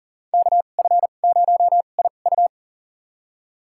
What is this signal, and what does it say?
Morse code, heard as a keyed tone, about 40 wpm, spaced as for 29 wpm, 700 Hz. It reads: KF0IU
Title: USA Call Signs - 5 Characters - 2x Character Spacing - 40wpm